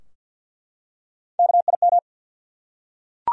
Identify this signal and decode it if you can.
Morse code, heard as a keyed tone, about 51 wpm, 700 Hz. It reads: KIM
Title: USA Common Names - Max 4 Letters - 50wpm